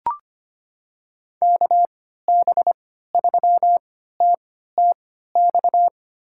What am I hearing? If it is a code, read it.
Morse code, heard as a keyed tone, about 25 wpm, spaced as for 14 wpm, 700 Hz. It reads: KB3TTX
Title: USA Call Signs - 6 Characters - 3x Character Spacing - 25wpm